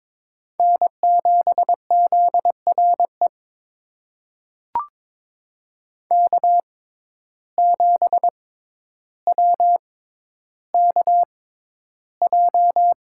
Morse code, heard as a keyed tone, about 23 wpm, spaced as for 11 wpm, 700 Hz. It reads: N7ZRE K7WKJ
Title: USA Call Signs - 5 Characters - 6x Character Spacing - 22wpm